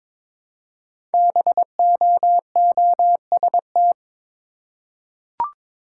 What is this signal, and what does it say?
Morse code, heard as a keyed tone, about 22 wpm, 700 Hz. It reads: BOOST
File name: 5 Letter Words - First 3 Letters Given - 22wpm